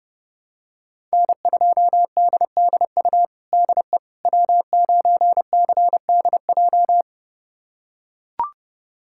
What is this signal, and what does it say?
Morse code, heard as a keyed tone, about 30 wpm, 700 Hz. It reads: N2DDU DE W9CDJ
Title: USA Callsign DE Callsign - 5 Characters - 30wpm